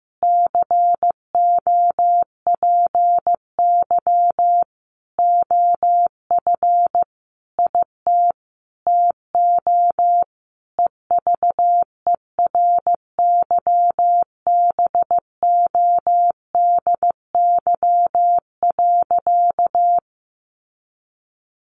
Morse code, heard as a keyed tone, about 15 wpm, 700 Hz. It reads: COPY OF IT TO EVERYBODY.
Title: Sentences from Top 1400 Words 15wpm